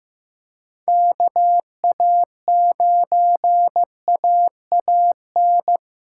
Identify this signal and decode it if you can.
Morse code, heard as a keyed tone, about 15 wpm, 700 Hz. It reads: KA9AAN
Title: USA Callsign DE Callsign 15wpm